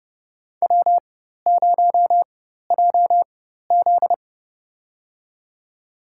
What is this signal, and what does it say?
Morse code, heard as a keyed tone, about 32 wpm, spaced as for 14 wpm, 700 Hz. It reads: W0JZ
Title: USA Call Signs - 4 Characters - 4x Character Spacing - 30wpm